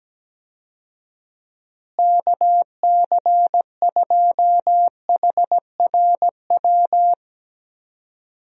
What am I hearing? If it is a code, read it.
Morse code, heard as a keyed tone, about 17 wpm, 700 Hz. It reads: KC2HRW
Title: USA Call Signs - 6 Characters - Repeated 3x - 17wpm